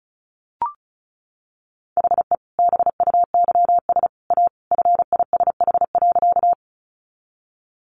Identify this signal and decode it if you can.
Morse code, heard as a keyed tone, about 34 wpm, 700 Hz. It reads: HE BUYS A FISH.
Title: Additional Easy Phrases 35wpm